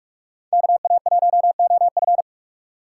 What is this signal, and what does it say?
Morse code, heard as a keyed tone, about 45 wpm, 700 Hz. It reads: KA1OF